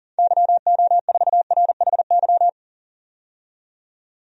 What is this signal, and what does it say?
Morse code, heard as a keyed tone, about 40 wpm, 700 Hz. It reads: YO4RHY
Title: DXCC-100 Call Signs - 6 Characters - 40wpm